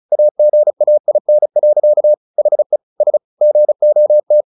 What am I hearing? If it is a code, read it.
Morse code, heard as a keyed tone, about 34 wpm, 600 Hz. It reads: AGAIN. HE S GOT
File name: the old man and the sea / the old man and the sea 1 35wpm